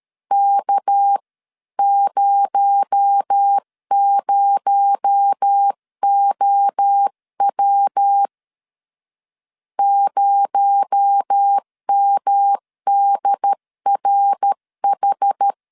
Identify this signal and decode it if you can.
Morse code, heard as a keyed tone, about 13 wpm, 800 Hz. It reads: K00OW 0MDRH
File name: CW / DET / 1 6 1500C S C 12wpm 3533